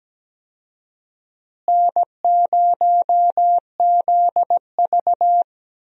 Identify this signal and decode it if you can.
Morse code, heard as a keyed tone, about 17 wpm, 700 Hz. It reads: N0ZV
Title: USA Call Signs - 4 Characters - Repeated 2x - 17wpm